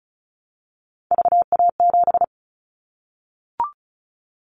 Morse code, heard as a keyed tone, about 36 wpm, 700 Hz. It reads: VA7
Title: Callsign Prefixes for Top 25 DXCC - 35wpm